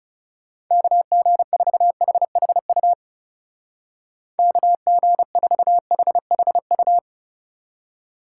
Morse code, heard as a keyed tone, about 38 wpm, 700 Hz. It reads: KG4HHU KG4HHU